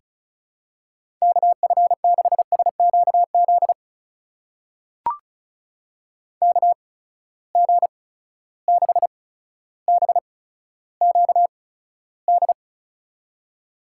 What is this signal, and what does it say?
Morse code, heard as a keyed tone, about 36 wpm, spaced as for 15 wpm, 700 Hz. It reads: KF6SQZ KG6BQD
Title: USA Call Signs - 6 Characters - 8x Character Spacing - 35wpm